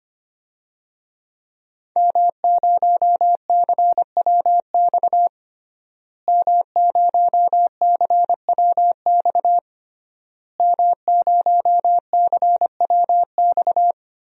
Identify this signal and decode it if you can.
Morse code, heard as a keyed tone, about 25 wpm, 700 Hz. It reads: M0CWX M0CWX M0CWX
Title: DXCC-100 Call Signs - 5 Characters - Repeated 3x - 25wpm